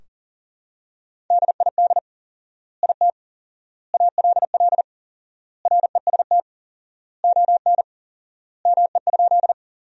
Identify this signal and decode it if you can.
Morse code, heard as a keyed tone, about 40 wpm, 700 Hz. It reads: DID IT ALL REST ON ME?
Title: 4x Word-Spacing - Sentences from Top 500 Words - 40wpm